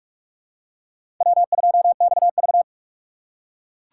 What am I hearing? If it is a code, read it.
Morse code, heard as a keyed tone, about 44 wpm, 700 Hz. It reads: W2XV
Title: USA Call Signs - 4 Characters - 3x Character Spacing - 45wpm